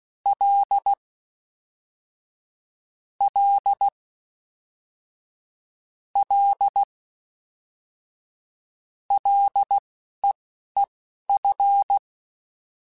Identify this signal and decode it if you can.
Morse code, heard as a keyed tone, about 16 wpm, spaced as for 12 wpm, 800 Hz. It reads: L L L LEEF
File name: Ls / Single L m3s / Single L Prefixe 3x2 Twice @10